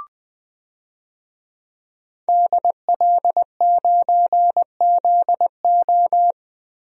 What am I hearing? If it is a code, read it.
Morse code, heard as a keyed tone, about 19 wpm, 700 Hz. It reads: DL9ZO
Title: DXCC-100 Call Signs - 5 Characters - Repeated 2x - 20wpm